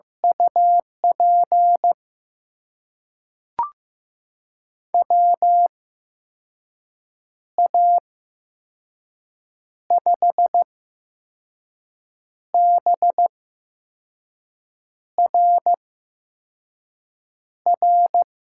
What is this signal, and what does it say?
Morse code, heard as a keyed tone, about 15 wpm, spaced as for 5 wpm, 700 Hz. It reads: UP WA5BRR
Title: USA Call Signs - 6 Characters - 8x Character Spacing - 15wpm